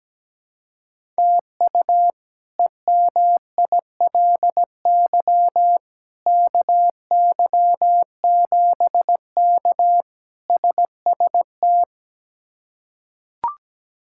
Morse code, heard as a keyed tone, about 17 wpm, 700 Hz. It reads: TU EMILY KY7K SST
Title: K1USN SST QSO Practice - 17wpm